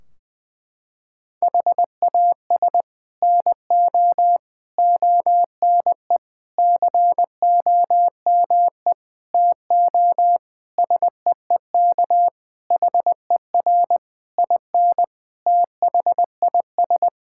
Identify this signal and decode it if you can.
Morse code, heard as a keyed tone, about 20 wpm, 700 Hz. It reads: HAS NO ONE COME TO SEEK HER IN THIS
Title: Sentences from Top 1600 Words 20wpm